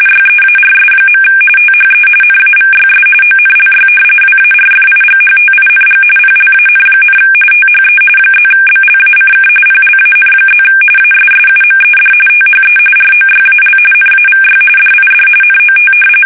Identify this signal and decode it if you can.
RTTY, about 45.45 baud. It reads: YEARS. THE NAVYS ONLY WARFIGHTING PLAN, THE +BLACK PLAN+ ASSUMED THE ROYAL NAVY DID NOT
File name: fsk850